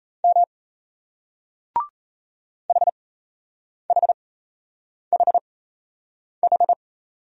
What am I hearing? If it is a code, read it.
Morse code, heard as a keyed tone, about 38 wpm, spaced as for 8 wpm, 700 Hz. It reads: M HHHH